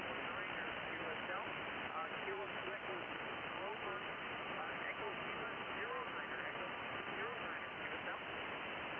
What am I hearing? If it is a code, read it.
FT8, with speech.